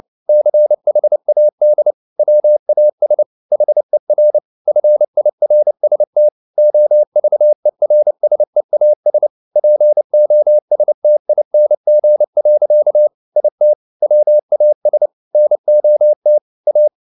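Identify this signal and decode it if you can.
Morse code, heard as a keyed tone, about 29 wpm, 600 Hz. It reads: CHAD WAS HER FIRST OVERSEAS POSTING. IT WAS NOT A
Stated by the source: Morse code at 29 wpm, sending Ken Follett's novel Never in English, from CW med Gnister.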